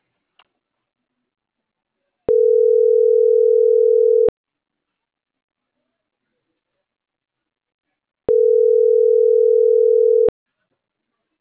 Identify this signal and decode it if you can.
Ringback tone (North America), 440 + 480 Hz.
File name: CENTRAL NMP / out marianacruznmp